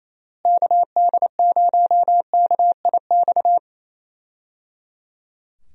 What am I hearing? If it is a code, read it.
Morse code, heard as a keyed tone, about 28 wpm, 700 Hz. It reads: KD0KIX
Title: USA Call Signs - 6 Characters - Repeated 3x - 28wpm